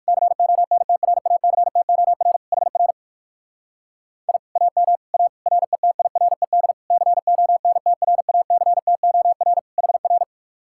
Morse code, heard as a keyed tone, about 67 wpm, spaced as for 57 wpm, 700 Hz. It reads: CONTRACTOR HR I AM A RETIRED CONTRACTOR HR